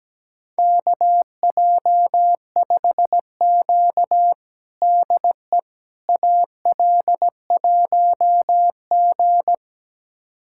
Morse code, heard as a keyed tone, about 17 wpm, 700 Hz. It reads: KJ5Q DE AL1G
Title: USA Callsign DE Callsign - 4 Characters - 17wpm